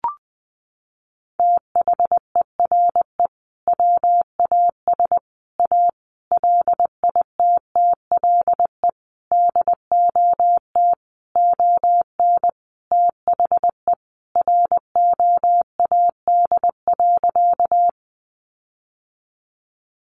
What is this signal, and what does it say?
Morse code, heard as a keyed tone, about 20 wpm, 700 Hz. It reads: THERE WAS A LITTLE DOT ON THE ROAD.
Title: Sentences from Top 1500 Words 20wpm